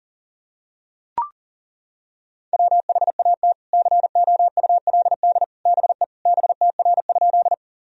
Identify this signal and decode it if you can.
Morse code, heard as a keyed tone, about 40 wpm, 700 Hz. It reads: WHAT COULD BE BTR?
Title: Sentences from Top 400 Words and Abbreviations 40wpm